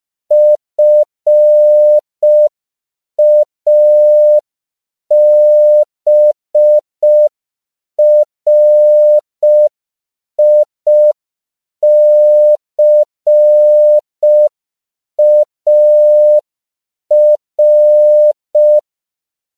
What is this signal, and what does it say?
Morse code, heard as a keyed tone, about 5 wpm, 600 Hz. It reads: FABRICAR
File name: 5PPM